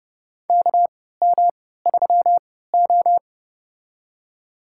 Morse code, heard as a keyed tone, about 28 wpm, spaced as for 17 wpm, 700 Hz. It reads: KM3O
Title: USA Call Signs - 4 Characters - 3x Character Spacing - 30wpm